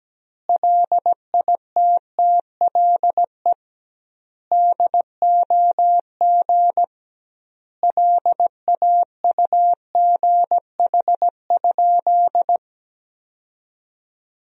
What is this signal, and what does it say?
Morse code, heard as a keyed tone, about 17 wpm, 700 Hz. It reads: LITTLE DOG LAUGH?
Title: 2x Word-Spacing - Sentences from Top 500 Words - 17wpm